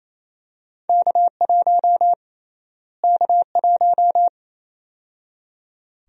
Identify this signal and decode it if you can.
Morse code, heard as a keyed tone, about 28 wpm, 700 Hz. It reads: K1 K1
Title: USA Call Sign Prefixes - Repeated 2x - 28wpm